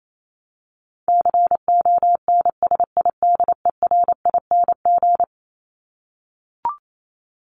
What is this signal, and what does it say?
Morse code, heard as a keyed tone, about 28 wpm, 700 Hz. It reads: CONSIDERING
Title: ING Suffix - Top 100 Words with Base Words 28wpm